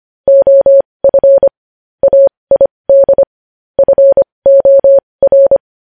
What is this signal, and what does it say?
Morse code, heard as a keyed tone, about 25 wpm, spaced as for 20 wpm, 550 Hz. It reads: OF AID FOR
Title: News Headlines at 20